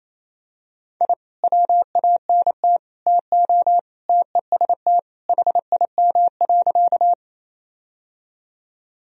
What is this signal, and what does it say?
Morse code, heard as a keyed tone, about 27 wpm, 700 Hz. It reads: I WANT TO TEST HIM.
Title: Sentences from Top 1000 Words - Review Part 3 - 28wpm